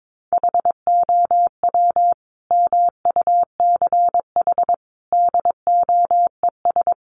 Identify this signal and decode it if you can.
Morse code, heard as a keyed tone, about 22 wpm, 700 Hz. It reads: HOW MUCH DOES
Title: Sentences from Top 800 Words 22wpm